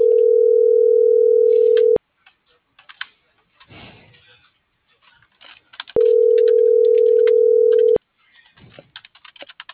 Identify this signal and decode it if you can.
Ringback tone (North America), 440 + 480 Hz.